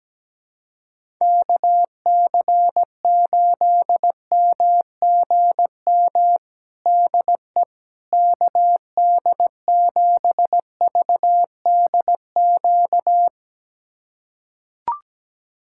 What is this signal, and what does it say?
Morse code, heard as a keyed tone, about 17 wpm, 700 Hz. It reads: KC8MGM DE KD7VDQ
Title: USA Callsign DE Callsign - 6 Characters - 17wpm